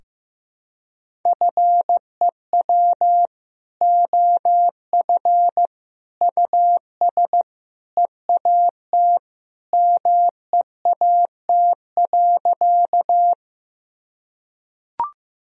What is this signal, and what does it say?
Morse code, heard as a keyed tone, about 16 wpm, 700 Hz. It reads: FEW OF US EAT MEAT.
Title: Sentences from Top 1000 Words - Review Part 3 - 15wpm